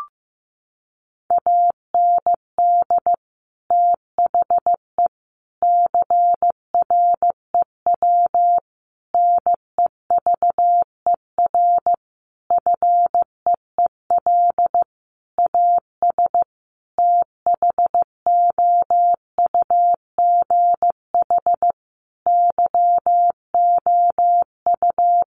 Morse code, heard as a keyed tone, about 15 wpm, 700 Hz. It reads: AND THE CREW NEVER FEEL AS THOUGH YOU
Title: Sentences from Top 1500 Words 15wpm